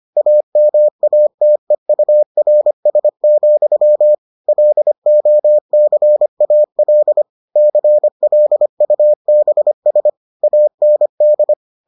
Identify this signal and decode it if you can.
Morse code, heard as a keyed tone, about 25 wpm, 600 Hz. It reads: AMATEURS, LOCAL CLUBS AND